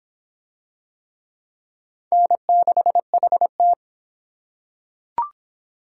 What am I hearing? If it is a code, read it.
Morse code, heard as a keyed tone, about 23 wpm, 700 Hz. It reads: N6HT